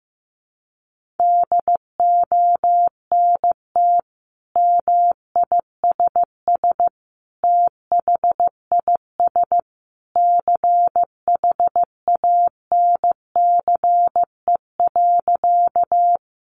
Morse code, heard as a keyed tone, about 15 wpm, 700 Hz. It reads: DONT MISS THIS CHANCE.
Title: Sentences from Top 800 Words 15wpm